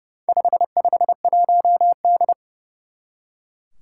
Morse code, heard as a keyed tone, about 30 wpm, 700 Hz. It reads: 551D
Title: Four Challenging Letter-Number Combinations 30wpm